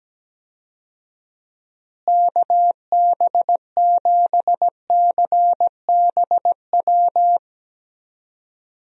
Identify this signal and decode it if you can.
Morse code, heard as a keyed tone, about 17 wpm, 700 Hz. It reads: KB7CBW
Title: USA Call Signs - 6 Characters - Repeated 3x - 17wpm